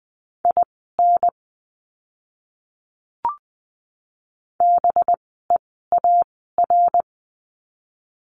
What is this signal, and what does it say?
Morse code, heard as a keyed tone, about 20 wpm, spaced as for 15 wpm, 700 Hz. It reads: IN BEAR
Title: 185 Lesson 30f - Letter b - Words 20wpm